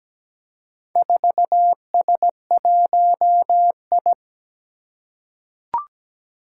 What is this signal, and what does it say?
Morse code, heard as a keyed tone, about 17 wpm, 700 Hz. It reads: 4S1I
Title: Four Challenging Letter-Number Combinations 17wpm